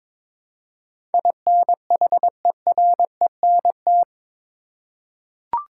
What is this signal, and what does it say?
Morse code, heard as a keyed tone, about 22 wpm, 700 Hz. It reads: INHERENT
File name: practice / 22wpm / Eight Letter Words - 22wpm